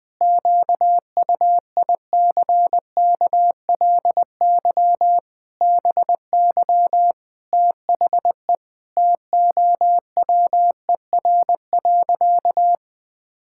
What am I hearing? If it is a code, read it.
Morse code, heard as a keyed tone, about 20 wpm, 700 Hz. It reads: QUICKLY BY THE TOWER.
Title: Pangram Challenge v2 - 20wpm